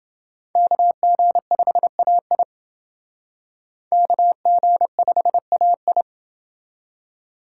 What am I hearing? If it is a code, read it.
Morse code, heard as a keyed tone, about 28 wpm, 700 Hz. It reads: KG5AI KG5AI